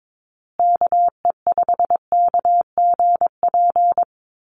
Morse code, heard as a keyed tone, about 22 wpm, 700 Hz. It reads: KE5KGP
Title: USA Call Signs Repeated 3x 22wpm